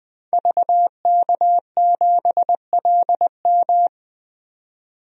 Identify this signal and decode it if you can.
Morse code, heard as a keyed tone, about 20 wpm, 700 Hz. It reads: VK7LM
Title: DXCC-100 Call Signs - 5 Characters - 20wpm